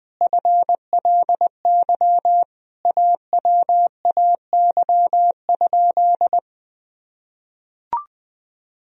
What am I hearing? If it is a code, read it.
Morse code, heard as a keyed tone, about 20 wpm, 700 Hz. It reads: FLY AWAY?
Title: Sentences from Top 1000 Words - Review Part 2 - 20wpm